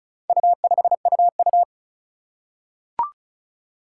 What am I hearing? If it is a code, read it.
Morse code, heard as a keyed tone, about 36 wpm, 700 Hz. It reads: U5UU
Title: Four Challenging Letter-Number Combinations 35wpm